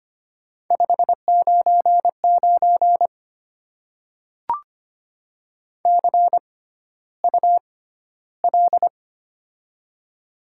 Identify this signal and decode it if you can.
Morse code, heard as a keyed tone, about 25 wpm, spaced as for 13 wpm, 690 Hz. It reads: 599 CUL
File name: Top 90 QSO Elements - 6x Character Spacing - 25wpm